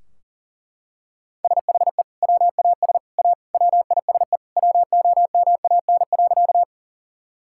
Morse code, heard as a keyed tone, about 40 wpm, 700 Hz. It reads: SHE WAS A WISE WOMAN.
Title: Sentences from Top 1600 Words 40wpm